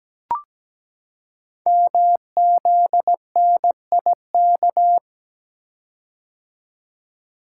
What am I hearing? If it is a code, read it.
Morse code, heard as a keyed tone, about 17 wpm, 700 Hz. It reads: MZNIK
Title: Code Groups - 5 Letters - 17wpm